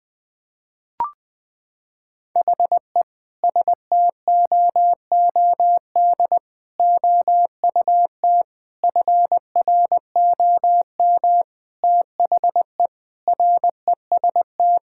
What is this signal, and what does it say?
Morse code, heard as a keyed tone, about 20 wpm, 700 Hz. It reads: HE STOOD OUT FROM THE REST
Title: Sentences with 5-Word Phrases - 20wpm